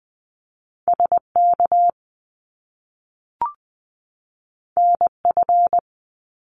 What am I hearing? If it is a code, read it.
Morse code, heard as a keyed tone, about 20 wpm, 700 Hz. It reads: SK NF
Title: Top Two Letter Combinations in Words 20wpm